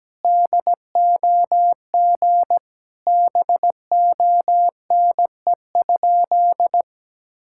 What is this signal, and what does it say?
Morse code, heard as a keyed tone, about 17 wpm, 700 Hz. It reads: DOG BONE?
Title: Sentences from Top 1600 Words 17wpm